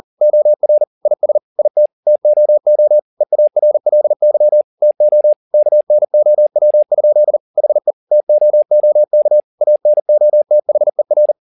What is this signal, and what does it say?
Morse code, heard as a keyed tone, about 40 wpm, 600 Hz. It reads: OR IS IT TOO EARLY TO KNOW? HE TOOK ANOTHER